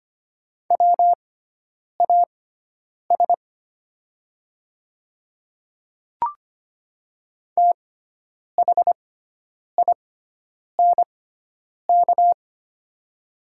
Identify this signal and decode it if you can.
Morse code, heard as a keyed tone, about 25 wpm, spaced as for 9 wpm, 700 Hz. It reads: WAS THINK
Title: 6x Character Spacing - Top 200 Words - 25wpm